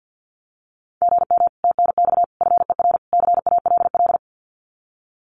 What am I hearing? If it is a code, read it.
Morse code, heard as a keyed tone, about 50 wpm, 700 Hz. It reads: GM TNX FER CALL
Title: Common QSO Phrases Encore 50wpm